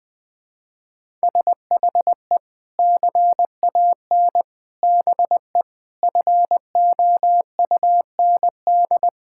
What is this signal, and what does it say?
Morse code, heard as a keyed tone, about 21 wpm, 700 Hz. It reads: SHE CAN BE FOUND
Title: Sentences from Top 1000 Words - Review Part 6 - 20wpm